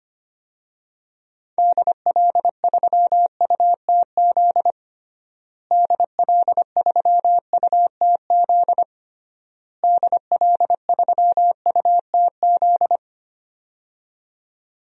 Morse code, heard as a keyed tone, about 25 wpm, 700 Hz. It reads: DL3UTZ DL3UTZ DL3UTZ
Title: DXCC-100 Call Signs - 6 Characters - Repeated 3x - 25wpm